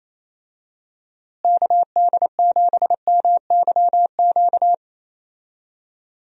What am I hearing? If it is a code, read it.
Morse code, heard as a keyed tone, about 27 wpm, 700 Hz. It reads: KD7MYQ
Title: USA Call Signs - 6 Characters - Repeated 3x - 28wpm